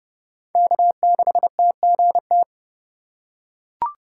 Morse code, heard as a keyed tone, about 30 wpm, 700 Hz. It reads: K6TGT
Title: USA Call Signs - 2x Character Spacing - 30wpm